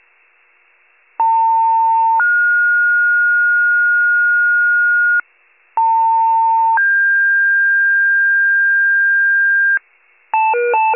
Two-tone page, heard as a one-second tone, then a three-second tone, about 900 then 1430 Hz, about 900 then 1600 Hz, with noise. It is heard from 1.2 to 9.8 s.